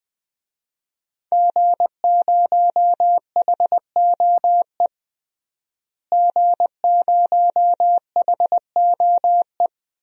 Morse code, heard as a keyed tone, about 20 wpm, 700 Hz. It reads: G0HOE G0HOE
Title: DXCC-100 Call Signs - 5 Characters - Repeated 3x - 20wpm